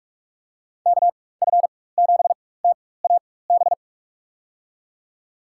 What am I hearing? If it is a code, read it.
Morse code, heard as a keyed tone, about 46 wpm, spaced as for 21 wpm, 700 Hz. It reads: KF7TAB
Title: USA Call Signs - 6 Characters - 4x Character Spacing - 45wpm